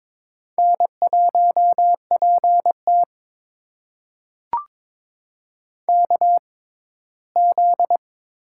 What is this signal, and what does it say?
Morse code, heard as a keyed tone, about 21 wpm, spaced as for 15 wpm, 700 Hz. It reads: N1PT KZ
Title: USA Call Signs - 4 Characters - 6x Character Spacing - 22wpm